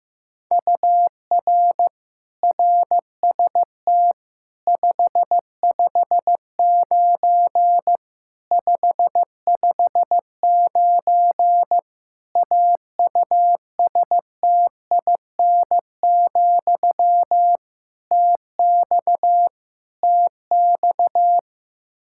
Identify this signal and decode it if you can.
Morse code, heard as a keyed tone, about 16 wpm, 700 Hz. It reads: UR RST 559 559 AUSTIN, TX TX
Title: K1USN SST QSO Practice 15wpm